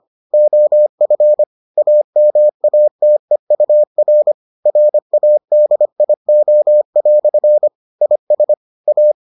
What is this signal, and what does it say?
Morse code, heard as a keyed tone, about 25 wpm, 600 Hz. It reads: OF AMATEUR RADIO" IS A